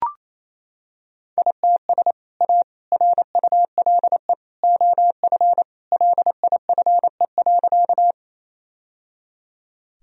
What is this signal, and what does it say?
Morse code, heard as a keyed tone, about 28 wpm, 700 Hz. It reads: ITS A RULE OF LIFE.
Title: Sentences from Top 1000 Words - Review Part 2 - 28wpm